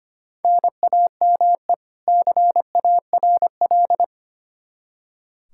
Morse code, heard as a keyed tone, about 25 wpm, 700 Hz. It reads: NAME CARL
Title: Practice QSO - Exchange 1 - Beginning - 25wpm